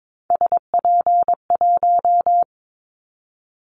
Morse code, heard as a keyed tone, about 22 wpm, 700 Hz. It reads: SP1
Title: Callsign Prefixes for Top 25 DXCC - 22wpm